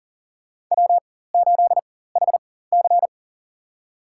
Morse code, heard as a keyed tone, about 38 wpm, spaced as for 18 wpm, 700 Hz. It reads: W8HC
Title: USA Call Signs - 4x Character Spacing - 40wpm